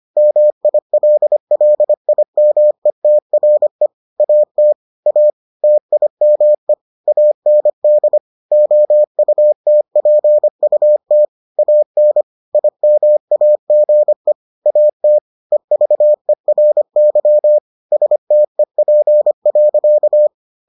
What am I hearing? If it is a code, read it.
Morse code, heard as a keyed tone, about 25 wpm, 600 Hz. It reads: MILLIMETRE AT A TIME AND OUTPUT AN IMAGE AT EVERY STEP.